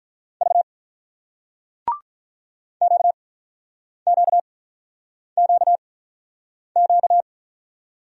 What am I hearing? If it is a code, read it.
Morse code, heard as a keyed tone, about 48 wpm, spaced as for 9 wpm, 700 Hz. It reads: V QQQQ